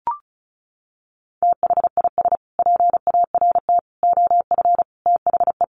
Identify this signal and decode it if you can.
Morse code, heard as a keyed tone, about 34 wpm, 700 Hz. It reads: THIS PART OF THE
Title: Sets of 5 Words Encore2 35wpm